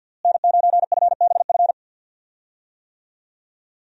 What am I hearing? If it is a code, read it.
Morse code, heard as a keyed tone, about 51 wpm, 700 Hz. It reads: N9FBF
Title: USA Call Signs - 5 Characters - Repeated 2x - 50wpm